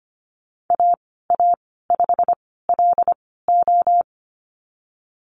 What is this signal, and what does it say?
Morse code, heard as a keyed tone, about 25 wpm, spaced as for 16 wpm, 700 Hz. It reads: AA5LO
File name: Lesson 15g - Number 5 - Call Signs 25wpm